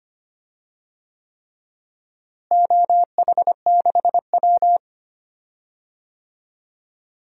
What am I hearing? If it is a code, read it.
Morse code, heard as a keyed tone, about 25 wpm, 700 Hz. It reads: OH6W
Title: DXCC-100 Call Signs - 4 Characters - Repeated 3x - 25wpm